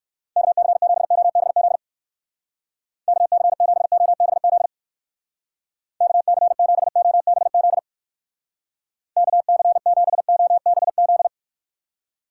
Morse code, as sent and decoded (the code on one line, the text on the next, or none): -.- -.-. --... --- -... --.. / -.- -.-. --... --- -... --.. / -.- -.-. --... --- -... --.. / -.- -.-. --... --- -... --..
KC7OBZ KC7OBZ KC7OBZ KC7OBZ